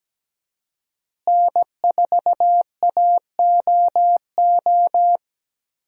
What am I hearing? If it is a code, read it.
Morse code, heard as a keyed tone, about 17 wpm, 700 Hz. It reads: N4AOO
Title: USA Call Signs - 5 Characters - Repeated 2x - 17wpm